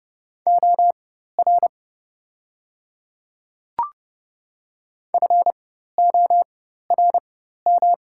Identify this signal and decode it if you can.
Morse code, heard as a keyed tone, about 30 wpm, spaced as for 14 wpm, 700 Hz. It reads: OR FORM
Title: 4x Character Spacing - Top 200 Words - 30wpm